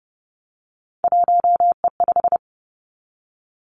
Morse code, heard as a keyed tone, about 32 wpm, 700 Hz. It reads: 1E5
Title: Three Challenging Letter-Number Combinations 30wpm